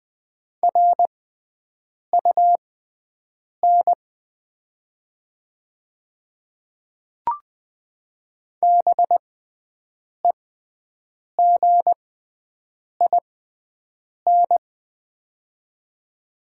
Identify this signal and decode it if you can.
Morse code, heard as a keyed tone, about 20 wpm, spaced as for 7 wpm, 700 Hz. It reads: RUN BEGIN
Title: 6x Character Spacing - Top 300 Words - 20wpm